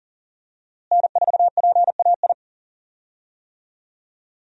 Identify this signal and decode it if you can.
Morse code, heard as a keyed tone, about 40 wpm, 700 Hz. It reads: N4PAI
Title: USA Call Signs - 5 Characters - Repeated 2x - 40wpm